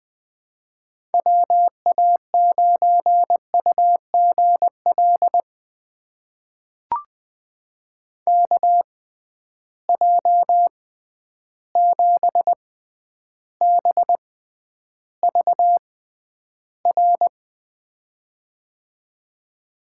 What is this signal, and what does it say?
Morse code, heard as a keyed tone, about 20 wpm, spaced as for 10 wpm, 700 Hz. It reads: WA9UGL KJ7BVR
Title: USA Call Signs - 6 Characters - 6x Character Spacing - 20wpm